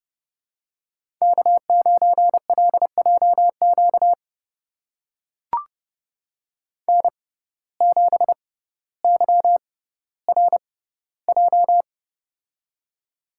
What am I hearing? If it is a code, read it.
Morse code, heard as a keyed tone, about 31 wpm, spaced as for 15 wpm, 700 Hz. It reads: K9LJQ N7YRJ
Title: USA Call Signs - 5 Characters - 6x Character Spacing - 30wpm